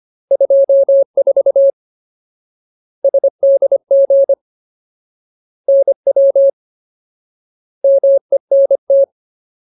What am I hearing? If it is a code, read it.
Morse code, heard as a keyed tone, about 25 wpm, 550 Hz. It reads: 24 SDG NW MENT
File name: Ultimate Word List / 25WPM Ultimate